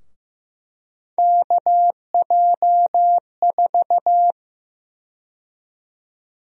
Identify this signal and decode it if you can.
Morse code, heard as a keyed tone, about 15 wpm, 700 Hz. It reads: KJ4
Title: USA Call Sign Prefixes - Repeated 2x - 15wpm